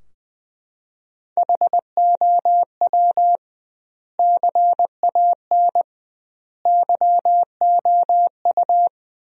Morse code, heard as a keyed tone, about 21 wpm, 700 Hz. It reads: HOW CAN YOU
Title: 2x Word-Spacing - Sentences from Top 100 Words - 20wpm